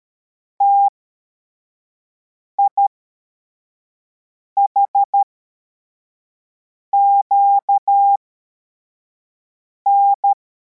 Morse code, heard as a keyed tone, about 13 wpm, spaced as for 4 wpm, 800 Hz. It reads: TIHQN
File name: formation / cw / Text81